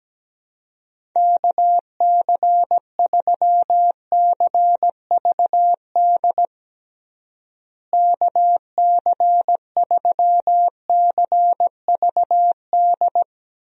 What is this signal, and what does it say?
Morse code, heard as a keyed tone, about 17 wpm, 700 Hz. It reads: KC3CVD KC3CVD
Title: USA Call Signs - 6 Characters - Repeated 3x - 17wpm